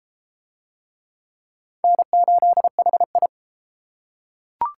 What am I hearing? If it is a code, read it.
Morse code, heard as a keyed tone, about 30 wpm, 700 Hz. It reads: N8HI